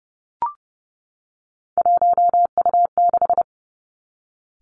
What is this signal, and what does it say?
Morse code, heard as a keyed tone, about 30 wpm, 700 Hz. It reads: 1U6
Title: Three Challenging Letter-Number Combinations 30wpm